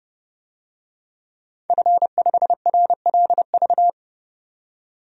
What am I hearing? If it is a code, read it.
Morse code, heard as a keyed tone, about 30 wpm, 700 Hz. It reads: F5RLV
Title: DXCC-100 Call Signs - 5 Characters - 30wpm